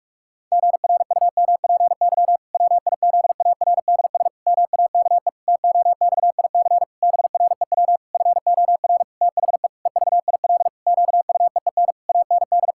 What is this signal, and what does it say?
Morse code, heard as a keyed tone, about 46 wpm, 700 Hz. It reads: GRUMPY WIZARDS MAKE TOXIC BREW FOR THE EVIL QUEEN AND
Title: Pangram Challenge v2 - 45wpm